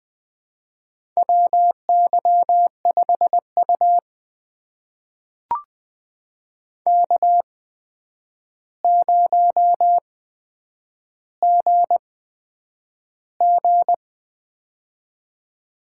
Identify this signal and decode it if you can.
Morse code, heard as a keyed tone, about 20 wpm, spaced as for 9 wpm, 700 Hz. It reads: WY5U K0GG
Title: USA Call Signs - 4 Characters - 8x Character Spacing - 20wpm